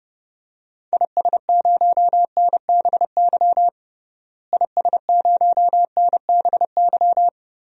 Morse code, heard as a keyed tone, about 30 wpm, 700 Hz. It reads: IS0NBY IS0NBY
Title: DXCC-100 Call Signs - 6 Characters - Repeated 3x - 30wpm